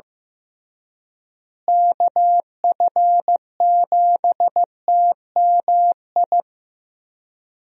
Morse code, heard as a keyed tone, about 16 wpm, 700 Hz. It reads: KF7TMI
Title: USA Call Signs - 6 Characters - Repeated 3x - 15wpm